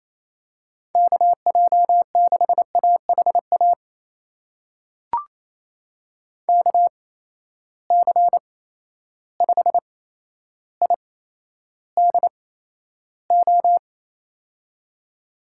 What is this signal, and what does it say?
Morse code, heard as a keyed tone, about 28 wpm, spaced as for 12 wpm, 700 Hz. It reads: KJ6AHA KC5IDO
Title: USA Call Signs - 6 Characters - 8x Character Spacing - 28wpm